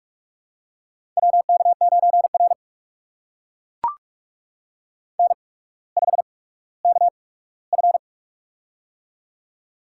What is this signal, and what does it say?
Morse code, heard as a keyed tone, about 44 wpm, spaced as for 19 wpm, 700 Hz. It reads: WK9R N5KF
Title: USA Call Signs - 4 Characters - 8x Character Spacing - 45wpm